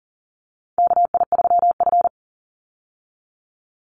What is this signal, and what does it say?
Morse code, heard as a keyed tone, about 38 wpm, 700 Hz. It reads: KI3F
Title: Call Signs Repeated 2x - 40wpm